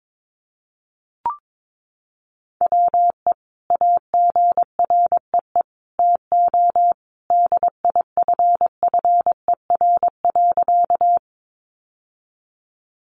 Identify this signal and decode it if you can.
Morse code, heard as a keyed tone, about 22 wpm, 700 Hz. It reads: WE AGREE TO DIFFER.
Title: Sentences from Top 1100 Words 22wpm